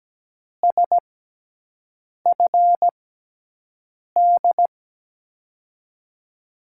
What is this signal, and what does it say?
Morse code, heard as a keyed tone, about 17 wpm, spaced as for 6 wpm, 700 Hz. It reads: SFD
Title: USA Call Signs - 6 Characters - 6x Character Spacing - 17wpm